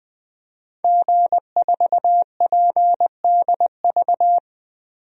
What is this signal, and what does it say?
Morse code, heard as a keyed tone, about 21 wpm, 700 Hz. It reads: G4PDV
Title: DXCC-100 Call Signs - 5 Characters - 20wpm